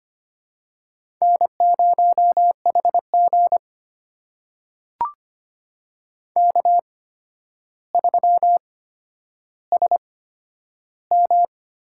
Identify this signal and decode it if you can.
Morse code, heard as a keyed tone, about 25 wpm, spaced as for 11 wpm, 700 Hz. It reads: N0HG K3SM
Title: USA Call Signs - 4 Characters - 8x Character Spacing - 25wpm